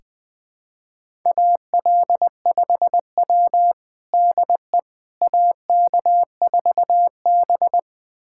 Morse code, heard as a keyed tone, about 20 wpm, 700 Hz. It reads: AL5W DE AK4B
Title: USA Callsign DE Callsign - 4 Characters - 20wpm